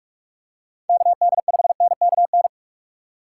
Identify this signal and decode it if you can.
Morse code, heard as a keyed tone, about 44 wpm, 700 Hz. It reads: KD5NKN
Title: USA Call Signs - 3x Character Spacing - 45wpm